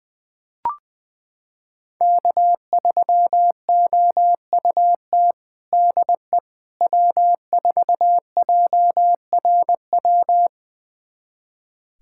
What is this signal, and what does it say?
Morse code, heard as a keyed tone, about 20 wpm, 700 Hz. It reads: K3OUT DE W4JRW
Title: USA Callsign DE Callsign - 5 Characters - 20wpm